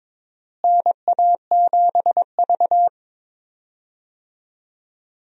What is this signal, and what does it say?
Morse code, heard as a keyed tone, about 22 wpm, 700 Hz. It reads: NA7V
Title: USA Call Signs - 4 Characters - Repeated 2x - 22wpm